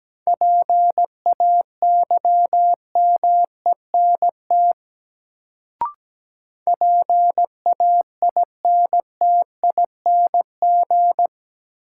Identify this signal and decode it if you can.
Morse code, heard as a keyed tone, about 17 wpm, 700 Hz. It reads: PAYMENT PAINTING
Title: Sentences from Top 1200 Words - New Word Drill Alt - 17wpm